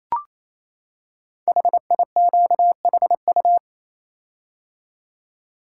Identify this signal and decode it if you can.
Morse code, heard as a keyed tone, about 27 wpm, 700 Hz. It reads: HIQHU
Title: Code Groups - 5 Letters - 28wpm